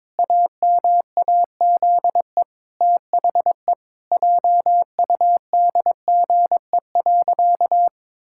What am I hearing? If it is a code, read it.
Morse code, heard as a keyed tone, about 22 wpm, 700 Hz. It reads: AMAZE THE JUDGE.
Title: Pangram Challenge v2 - 22wpm